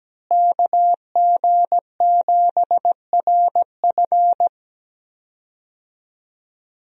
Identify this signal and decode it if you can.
Morse code, heard as a keyed tone, about 17 wpm, 700 Hz. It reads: KG7RF
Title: USA Call Signs - 5 Characters - Repeated 3x - 17wpm